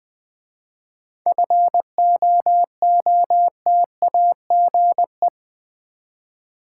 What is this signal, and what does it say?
Morse code, heard as a keyed tone, about 20 wpm, 700 Hz. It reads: FOOTAGE
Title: 7 Letter Words - First 3 Letters Given - 20wpm